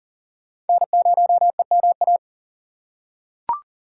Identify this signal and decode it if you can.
Morse code, heard as a keyed tone, about 40 wpm, 700 Hz. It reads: N0EMA
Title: USA Call Signs - 5 Characters - 4x Character Spacing - 40wpm